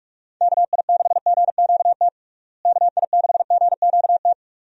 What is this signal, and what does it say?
Morse code, heard as a keyed tone, about 46 wpm, 700 Hz. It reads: KI6GQT KI6GQT
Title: USA Call Signs - 6 Characters - Repeated 2x - 45wpm